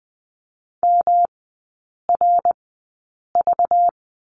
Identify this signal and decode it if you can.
Morse code, heard as a keyed tone, about 20 wpm, spaced as for 8 wpm, 690 Hz. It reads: MRV
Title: 191 Lesson 31e - Letter v - Three Characters 20wpm